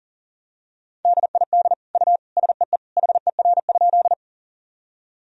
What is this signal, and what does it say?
Morse code, heard as a keyed tone, about 37 wpm, 700 Hz. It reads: DID U SEE HER?
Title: Sentences from Top 100 Words and Abbreviations 40wpm